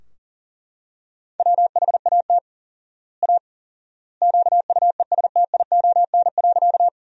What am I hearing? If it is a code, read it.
Morse code, heard as a keyed tone, about 40 wpm, 700 Hz. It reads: WHAT A QUESTION.
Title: 4x Word-Spacing - Sentences from Top 400 Words - 40wpm